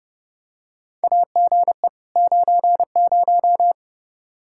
Morse code, heard as a keyed tone, about 30 wpm, 700 Hz. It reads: AGE 90
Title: Practice QSO - Exchange 3 - More Info - 30wpm